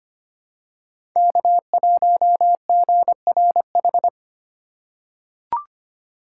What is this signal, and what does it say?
Morse code, heard as a keyed tone, about 25 wpm, 700 Hz. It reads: K1GRH
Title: USA Call Signs - 5 Characters - 4x Character Spacing - 25wpm